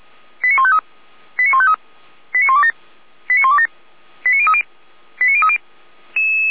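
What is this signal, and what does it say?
Five-tone selective call (ZVEI), heard as quick runs of five tones, tones of about 70 ms. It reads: 89153 89153 89127 89127 89030 89030
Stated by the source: Five-tone ZVEI selective call received on SDR.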